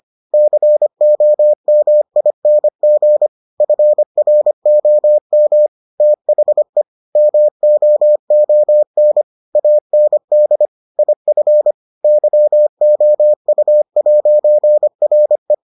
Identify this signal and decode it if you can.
Morse code, heard as a keyed tone, about 25 wpm, 600 Hz. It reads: COMING FROM THE MOON AND IF YOU'RE